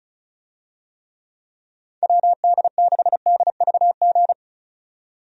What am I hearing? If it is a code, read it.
Morse code, heard as a keyed tone, about 34 wpm, 700 Hz. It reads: WD6DVG
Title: USA Call Signs - 6 Characters - 35wpm